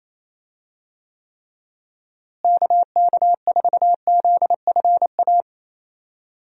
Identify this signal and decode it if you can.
Morse code, heard as a keyed tone, about 28 wpm, 700 Hz. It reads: KK4ZFA
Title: USA Call Signs - 6 Characters - Repeated 3x - 28wpm